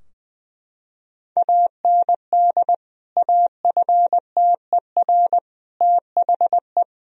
Morse code, heard as a keyed tone, about 20 wpm, 700 Hz. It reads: AND AFTER THE
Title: Sentences from Top 1000 Words - Review Part 2 - 20wpm